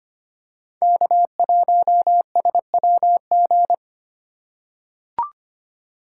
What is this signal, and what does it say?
Morse code, heard as a keyed tone, about 25 wpm, 700 Hz. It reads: K1SWG
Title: USA Call Signs - 5 Characters - 3x Character Spacing - 25wpm